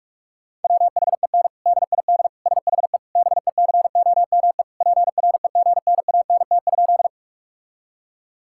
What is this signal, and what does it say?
Morse code, heard as a keyed tone, about 46 wpm, 690 Hz. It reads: WHEN DID SHE BECOME PREGNANT?
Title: Sentences from Top 1000 Words - Review Part 6 - 45wpm